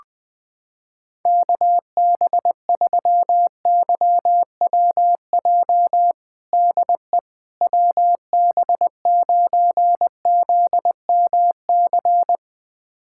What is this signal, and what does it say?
Morse code, heard as a keyed tone, about 21 wpm, 700 Hz. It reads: KB3YWJ DE WB9ZMC
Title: USA Callsign DE Callsign 20wpm